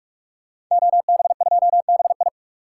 Morse code, heard as a keyed tone, about 42 wpm, 700 Hz. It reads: OB2BI
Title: Code Groups - 5 Letter:Numbers - 45wpm